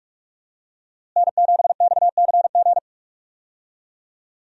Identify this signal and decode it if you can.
Morse code, heard as a keyed tone, about 42 wpm, 700 Hz. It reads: N7XCG
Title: USA Call Signs - 5 Characters - Repeated 3x - 45wpm